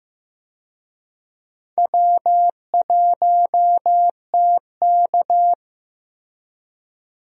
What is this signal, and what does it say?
Morse code, heard as a keyed tone, about 15 wpm, 700 Hz. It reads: W1TK
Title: USA Call Signs - 4 Characters - Repeated 2x - 15wpm